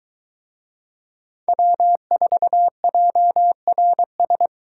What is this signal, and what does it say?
Morse code, heard as a keyed tone, about 23 wpm, 700 Hz. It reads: W4JRS